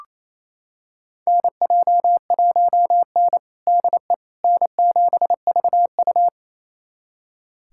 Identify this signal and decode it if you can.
Morse code, heard as a keyed tone, about 28 wpm, 700 Hz. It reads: NJ1N DE N7VU
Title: USA Callsign DE Callsign - 4 Characters - 28wpm